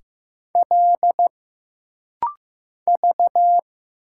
Morse code, heard as a keyed tone, about 15 wpm, 700 Hz. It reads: L V
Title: Single Letter-Number Alt - Rapid-Fire 15wpm